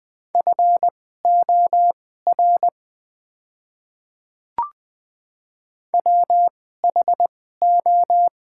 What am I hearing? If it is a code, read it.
Morse code, heard as a keyed tone, about 20 wpm, spaced as for 15 wpm, 700 Hz. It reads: FOR WHO
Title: 2x Character Spacing - Top 100 Words - 20wpm